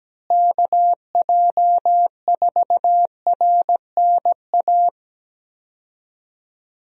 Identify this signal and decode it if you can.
Morse code, heard as a keyed tone, about 17 wpm, 700 Hz. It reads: KJ4RNA
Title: USA Call Signs - 6 Characters - Repeated 2x - 17wpm